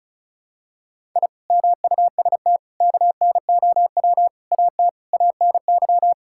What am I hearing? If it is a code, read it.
Morse code, heard as a keyed tone, about 36 wpm, 700 Hz. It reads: I MUST KNOW AT ANY
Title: Sentences from Top 1000 Words - Review Part 5 - 35wpm